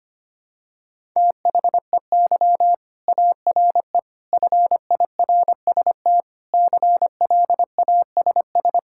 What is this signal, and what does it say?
Morse code, heard as a keyed tone, about 25 wpm, 700 Hz. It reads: THEY ARE FIRST CLASS
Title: Sentences from Top 1000 Words - Review Part 5 - 25wpm